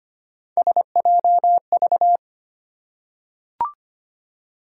Morse code, heard as a keyed tone, about 25 wpm, 700 Hz. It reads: SJV
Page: morse-code-ninja-practice-25wpm